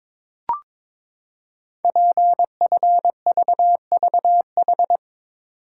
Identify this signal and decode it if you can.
Morse code, heard as a keyed tone, about 23 wpm, 700 Hz. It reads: PFVVH
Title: Code Groups - 5 Letters - 22wpm